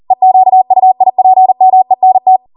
Morse code, heard as a keyed tone, about 40 wpm, 750 Hz. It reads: EQUIPMENT